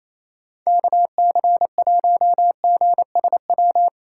Morse code, heard as a keyed tone, about 34 wpm, 700 Hz. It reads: KC1GSW
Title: Speed Racing - USA Call Signs 28wpm